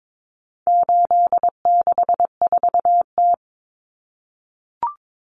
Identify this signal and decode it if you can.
Morse code, heard as a keyed tone, about 22 wpm, 700 Hz. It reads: 864T
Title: Common Cut Numbers - 4-Digit 22wpm